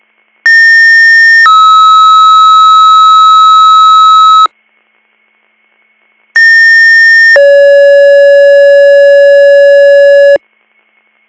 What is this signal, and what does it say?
Two-tone page, heard as a one-second tone, then a three-second tone, about 1800 then 1290 Hz, about 1800 then 560 Hz. It is heard from 0.5 to 10.4 s.